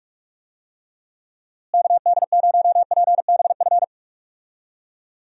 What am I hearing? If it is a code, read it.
Morse code, heard as a keyed tone, about 46 wpm, 700 Hz. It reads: KD0PBF
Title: USA Call Signs - 6 Characters - Repeated 3x - 45wpm